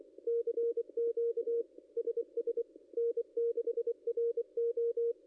Morse code, heard as a keyed tone, about 24 wpm, 470 Hz. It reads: CQ SS N6RO
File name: contests / ss-cw-k5tr-op-n5rz -0800